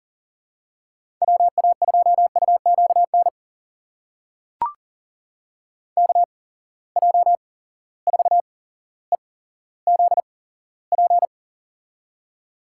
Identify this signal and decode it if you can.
Morse code, heard as a keyed tone, about 40 wpm, spaced as for 17 wpm, 700 Hz. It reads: WA2UQN KJ4EZP
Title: USA Call Signs - 8x Character Spacing - 40wpm